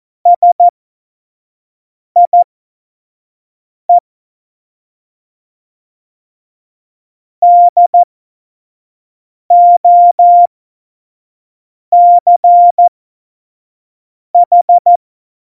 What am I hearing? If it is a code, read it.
Morse code, heard as a keyed tone, about 14 wpm, spaced as for 5 wpm, 700 Hz. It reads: SIE DOCH
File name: Morsekurs / Witze / Witz 06. Februar 2018, 25 F70